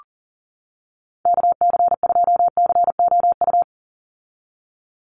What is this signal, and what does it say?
Morse code, heard as a keyed tone, about 40 wpm, 700 Hz. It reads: KC2COU
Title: USA Call Signs 40wpm